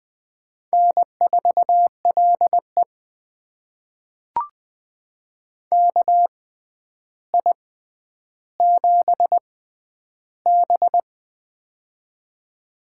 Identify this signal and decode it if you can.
Morse code, heard as a keyed tone, about 20 wpm, spaced as for 10 wpm, 700 Hz. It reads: N4LE KI7B
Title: USA Call Signs - 4 Characters - 6x Character Spacing - 20wpm